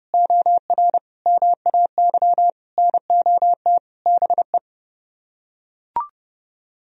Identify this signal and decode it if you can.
Morse code, heard as a keyed tone, about 28 wpm, 700 Hz. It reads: OR MAY NOT BE